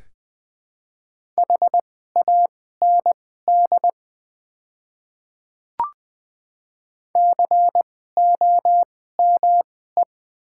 Morse code, heard as a keyed tone, about 21 wpm, spaced as for 15 wpm, 700 Hz. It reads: HAND COME
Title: 2x Character Spacing - Top 200 Words - 20wpm